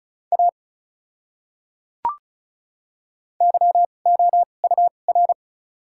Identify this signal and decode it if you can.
Morse code, heard as a keyed tone, about 36 wpm, spaced as for 25 wpm, 700 Hz. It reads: A YOUR
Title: 2x Character Spacing - Top 100 Words - 35wpm